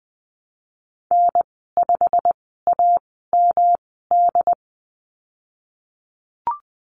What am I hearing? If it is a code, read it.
Morse code, heard as a keyed tone, about 20 wpm, spaced as for 15 wpm, 700 Hz. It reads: N5AMD
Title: 122 Lesson 20g - Number 3 - Call Signs 20wpm